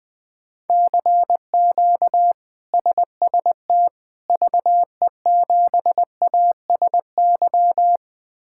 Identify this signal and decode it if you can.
Morse code, heard as a keyed tone, about 20 wpm, 700 Hz. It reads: CQ SST VE7ASY
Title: K1USN SST QSO Practice - 20wpm